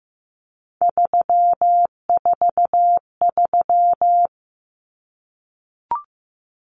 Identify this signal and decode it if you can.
Morse code, heard as a keyed tone, about 15 wpm, 700 Hz. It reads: 343
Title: Common Cut Numbers - 3-Digit 15wpm